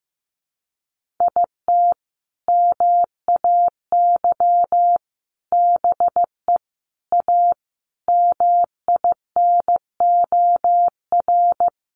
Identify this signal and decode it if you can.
Morse code, heard as a keyed tone, about 15 wpm, 700 Hz. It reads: IT MAY BE A MINOR
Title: Sentences from Top 1500 Words 15wpm